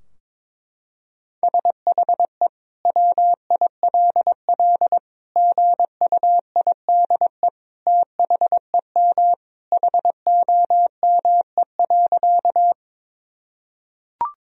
Morse code, heard as a keyed tone, about 22 wpm, 700 Hz. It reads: SHE WILL GUIDE THEM HOME.
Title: Sentences from Top 1000 Words - Review Part 3 - 22wpm